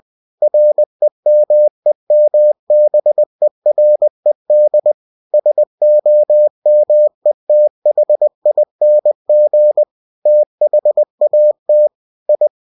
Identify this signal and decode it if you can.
Morse code, heard as a keyed tone, about 20 wpm, 600 Hz. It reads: REMEMBERED SOMETHING THAT I
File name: cw-books / winnie-the-pooh / winnie-20wpm-page-0024